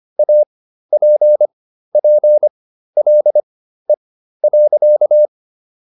Morse code, heard as a keyed tone, about 25 wpm, spaced as for 13 wpm, 600 Hz. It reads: APPLE.